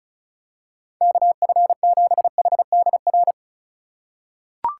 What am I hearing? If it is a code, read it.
Morse code, heard as a keyed tone, about 34 wpm, 700 Hz. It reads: KF7HDR
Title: USA Call Signs - 6 Characters - 2x Character Spacing - 35wpm